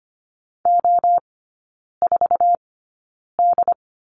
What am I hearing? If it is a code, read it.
Morse code, heard as a keyed tone, about 25 wpm, spaced as for 9 wpm, 700 Hz. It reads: O4D
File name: course / 25:10wpm / 093 Lesson 16e - Letter c - Three Characters 25wpm